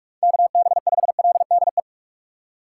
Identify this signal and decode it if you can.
Morse code, heard as a keyed tone, about 44 wpm, 700 Hz. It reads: KB5LDE
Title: USA Call Signs - 6 Characters - 8x Character Spacing - 45wpm